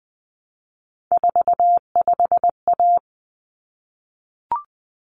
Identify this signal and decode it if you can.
Morse code, heard as a keyed tone, about 20 wpm, 700 Hz. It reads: 45A
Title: Common Cut Numbers - 3-Digit 20wpm